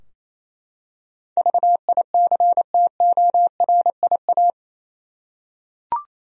Morse code, heard as a keyed tone, about 27 wpm, 700 Hz. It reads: VICTORIA
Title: USA Common Names - 28wpm